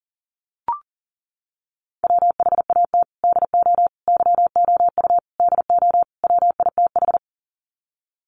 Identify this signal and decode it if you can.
Morse code, heard as a keyed tone, about 37 wpm, 700 Hz. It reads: WHAT DO YOU DO WITH
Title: Sets of 5 Words Encore2 40wpm